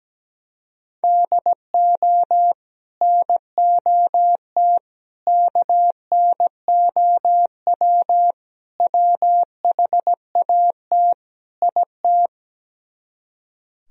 Morse code, heard as a keyed tone, about 17 wpm, 700 Hz. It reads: DO NOT KNOW WHAT IT
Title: Speed-Racing - Sets of 5 Words 17wpm